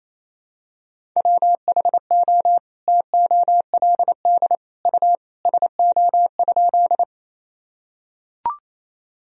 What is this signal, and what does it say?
Morse code, heard as a keyed tone, about 28 wpm, 700 Hz. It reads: WHO TOLD U SO?
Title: Sentences from Top 400 Words and Abbreviations 28wpm